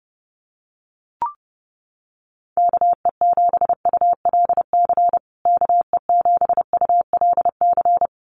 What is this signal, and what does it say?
Morse code, heard as a keyed tone, about 30 wpm, 700 Hz. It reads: KE7ULC KE7ULC
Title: USA Call Signs Repeated 2x 30wpm